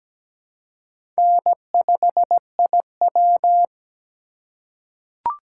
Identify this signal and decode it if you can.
Morse code, heard as a keyed tone, about 17 wpm, 700 Hz. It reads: N5IW
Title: USA Call Signs - 4 Characters - 3x Character Spacing - 17wpm